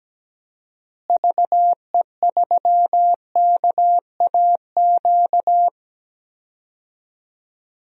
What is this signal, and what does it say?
Morse code, heard as a keyed tone, about 17 wpm, 700 Hz. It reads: VE3KAQ
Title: DXCC-100 Call Signs - 6 Characters - Repeated 2x - 17wpm